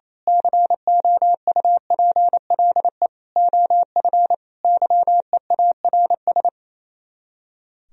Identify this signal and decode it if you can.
Morse code, heard as a keyed tone, about 27 wpm, 700 Hz. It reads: COUPLE OF YEARS
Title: Speed-Racing - Sets of 5 Words 28wpm